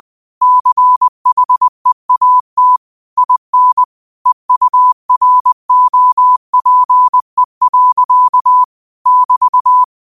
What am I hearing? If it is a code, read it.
Morse code, heard as a keyed tone, about 20 wpm, 1000 Hz. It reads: CHEAT IN EUROPE. <BT>
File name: qotdfiles / 20wpm